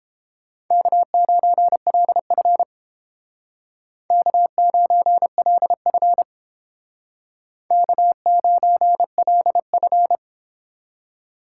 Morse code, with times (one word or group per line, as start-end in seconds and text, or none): 0.7-2.6 K9LF
4.1-6.2 K9LF
7.7-10.2 K9LF